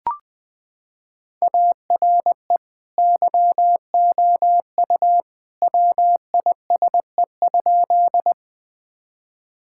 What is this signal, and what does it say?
Morse code, heard as a keyed tone, about 20 wpm, 700 Hz. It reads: ARE YOU WISE?
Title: Sentences from Top 1600 Words 20wpm